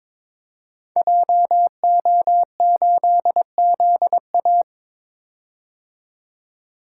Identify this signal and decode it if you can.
Morse code, heard as a keyed tone, about 22 wpm, 700 Hz. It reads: JO8ZA
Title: Code Groups - 5 Letter:Numbers - 22wpm